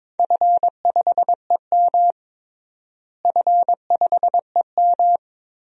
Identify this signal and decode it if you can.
Morse code, heard as a keyed tone, about 22 wpm, 700 Hz. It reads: F5EM F5EM
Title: DXCC-100 Call Signs - 4 Characters - Repeated 3x - 22wpm